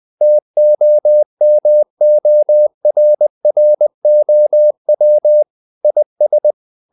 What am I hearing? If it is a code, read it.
Morse code, heard as a keyed tone, about 20 wpm, 600 Hz. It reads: TOMORROW IS
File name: the old man and the sea / the old man and the sea part 1 20wpm